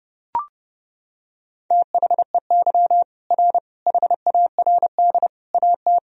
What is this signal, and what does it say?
Morse code, heard as a keyed tone, about 30 wpm, 700 Hz. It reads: THEY R HARD AT
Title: Sentences from Top 300 Words and Abbreviations 30wpm